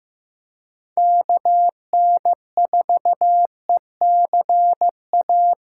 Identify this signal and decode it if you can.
Morse code, heard as a keyed tone, about 18 wpm, 700 Hz. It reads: KN4ECA
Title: Speed Racing - USA 2x3 Call Signs 15wpm